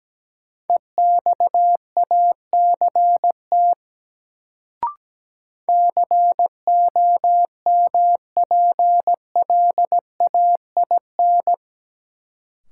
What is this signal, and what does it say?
Morse code, heard as a keyed tone, about 17 wpm, 700 Hz. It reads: EXACT COMPLAIN
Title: Sentences from Top 1300 Words - New Word Drill - 17wpm